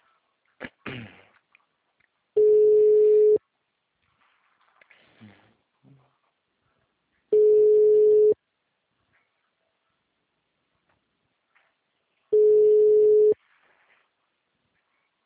Ringback tone (Europe, 425 Hz).
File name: out- -2022- - - .52349